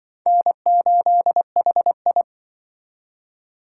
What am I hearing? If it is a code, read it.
Morse code, heard as a keyed tone, about 25 wpm, 700 Hz. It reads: N8HI